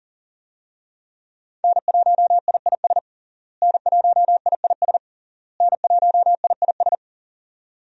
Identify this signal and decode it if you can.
Morse code, heard as a keyed tone, about 40 wpm, 700 Hz. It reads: N1IIS N1IIS N1IIS
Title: USA Call Signs - 5 Characters - Repeated 3x - 40wpm